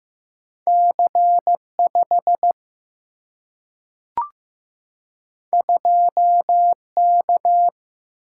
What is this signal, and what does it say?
Morse code, heard as a keyed tone, about 15 wpm, 700 Hz. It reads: C 5 2 K
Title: Single-Letter Single-Number 15wpm